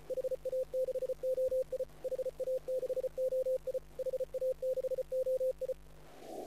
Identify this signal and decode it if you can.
Morse code, heard as a keyed tone, about 35 wpm, 530 Hz. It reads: HA6OI HA6OI HA6OI